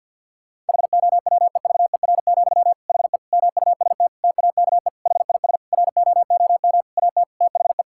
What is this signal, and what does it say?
Morse code, heard as a keyed tone, about 50 wpm, 700 Hz. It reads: HOWEVER, HE MUST TAKE HIS ROOM AT THE
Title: Sentences from Top 1600 Words 50wpm